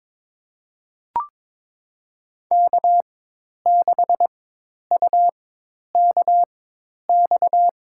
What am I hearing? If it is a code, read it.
Morse code, heard as a keyed tone, about 22 wpm, spaced as for 10 wpm, 700 Hz. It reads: K6UKX
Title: USA Call Signs - 5 Characters - 4x Character Spacing - 22wpm